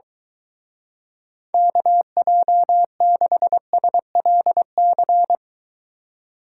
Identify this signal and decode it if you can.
Morse code, heard as a keyed tone, about 24 wpm, 700 Hz. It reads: KJ6SLC